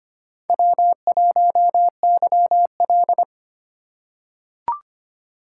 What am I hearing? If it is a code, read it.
Morse code, heard as a keyed tone, about 25 wpm, 700 Hz. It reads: W1YL
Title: USA Call Signs - 4 Characters - 3x Character Spacing - 25wpm